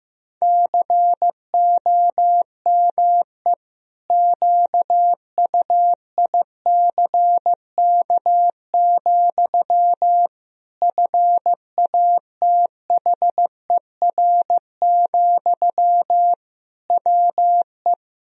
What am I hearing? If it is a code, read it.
Morse code, heard as a keyed tone, about 15 wpm, 700 Hz. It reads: COME QUICK, FATHER, WE
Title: Sentences from Top 1000 Words - Review Part 3 - 15wpm